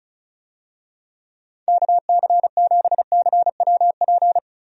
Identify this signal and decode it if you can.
Morse code, heard as a keyed tone, about 34 wpm, 700 Hz. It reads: KC7CWP
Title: USA Call Signs - 6 Characters - Repeated 3x - 35wpm